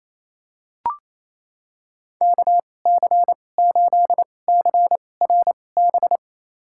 Morse code, heard as a keyed tone, about 28 wpm, spaced as for 20 wpm, 700 Hz. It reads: KC8CRB
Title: USA Call Signs - 6 Characters - 2x Character Spacing - 28wpm